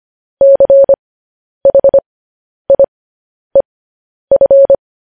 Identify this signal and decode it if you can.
Morse code, heard as a keyed tone, about 25 wpm, spaced as for 10 wpm, 550 Hz. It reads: CHIEF
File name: podcast / News headlines-1x-25x10